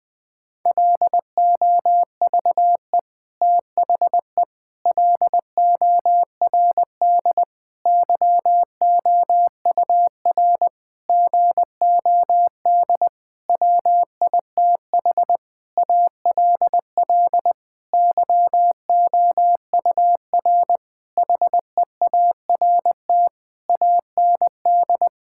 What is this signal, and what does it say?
Morse code, heard as a keyed tone, about 20 wpm, 700 Hz. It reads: LOVE THE LORD YOUR GOD WITH ALL YOUR HEART AND
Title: Sentences with 9-Word Phrases - 20wpm